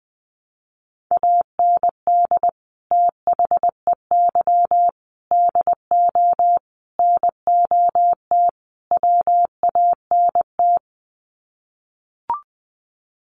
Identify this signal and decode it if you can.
Morse code, heard as a keyed tone, about 19 wpm, 700 Hz. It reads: AND THEY DO NOT WANT
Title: Sets of 5 Words Encore2 20wpm